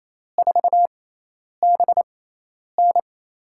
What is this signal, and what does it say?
Morse code, heard as a keyed tone, about 28 wpm, spaced as for 10 wpm, 700 Hz. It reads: 4BN
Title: USA Call Signs - 6x Character Spacing - 28wpm